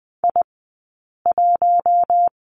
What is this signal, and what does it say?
Morse code, heard as a keyed tone, about 20 wpm, 700 Hz. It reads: I 1
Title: 046 Lesson 9d - Number 4 - Two Characters 20wpm